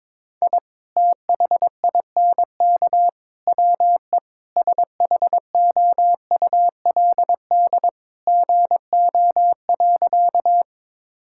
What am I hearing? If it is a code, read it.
Morse code, heard as a keyed tone, about 22 wpm, 700 Hz. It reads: I THINK WE SHOULD GO.
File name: practice / 22wpm / Sentences with 3-Word Phrases - 22wpm